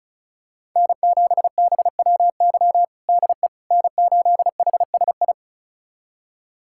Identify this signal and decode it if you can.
Morse code, heard as a keyed tone, about 36 wpm, 700 Hz. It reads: N7BWY DE N8HSI
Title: USA Callsign DE Callsign - 5 Characters - 35wpm